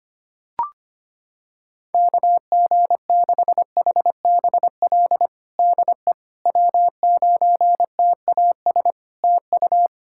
Morse code, heard as a keyed tone, about 25 wpm, 700 Hz. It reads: KG6HBL DE W9TAS TU
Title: Practice QSO - Exchange 4 - Ending - 25wpm